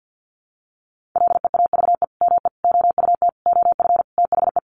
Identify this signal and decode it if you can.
Morse code, heard as a keyed tone, about 50 wpm, 700 Hz. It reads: LEAVE ME OUT OF THE
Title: Sentences from Top 400 Words 50wpm